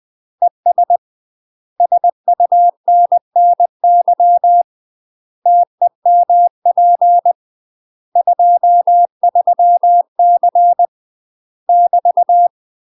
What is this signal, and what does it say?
Morse code, heard as a keyed tone, about 20 wpm, 700 Hz. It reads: ES SUNNY TEMP 23C <BT>
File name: UK MASTER QSOs TRAINING / 20WPM / UK MASTER QSOs 0025